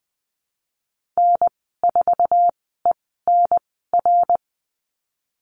Morse code, heard as a keyed tone, about 19 wpm, spaced as for 14 wpm, 700 Hz. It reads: N4ENR
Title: 055 Lesson 10g - Letter r - Call Signs 20wpm